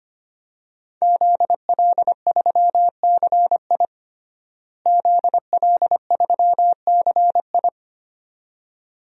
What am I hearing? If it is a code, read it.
Morse code, heard as a keyed tone, about 25 wpm, 700 Hz. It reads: ZL3CI ZL3CI
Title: DXCC-100 Call Signs - 5 Characters - Repeated 2x - 25wpm